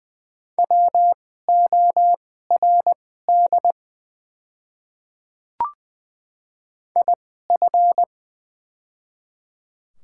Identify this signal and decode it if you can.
Morse code, heard as a keyed tone, about 20 wpm, spaced as for 14 wpm, 700 Hz. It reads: WORD IF
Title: 2x Character Spacing - Top 100 Words - 20wpm